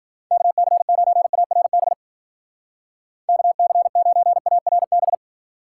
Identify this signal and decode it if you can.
Morse code, heard as a keyed tone, about 50 wpm, 700 Hz. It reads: KC9ARB KC9ARB
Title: Speed Racing - USA Call Signs 40wpm